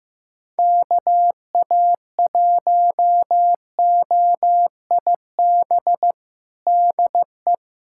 Morse code, heard as a keyed tone, about 15 wpm, 700 Hz. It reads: KA1OIB DE
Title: USA Callsign DE Callsign 15wpm